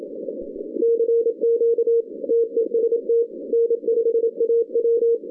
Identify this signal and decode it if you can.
Morse code, heard as a keyed tone, about 26 wpm, 460 Hz. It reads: CQ TEST N5AW
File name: 2015 cq ww cw / 2015 cqwww k5tr - 0-18